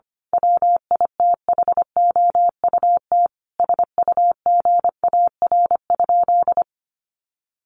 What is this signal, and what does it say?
Morse code, heard as a keyed tone, about 25 wpm, 700 Hz. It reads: WITHOUT SUGAR?
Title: Sentences from Top 1200 Words 25wpm